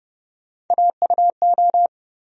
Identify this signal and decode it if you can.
Morse code, heard as a keyed tone, about 29 wpm, 700 Hz. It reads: AUO
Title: USA Call Sign Postfixes - 3 Letters 30wpm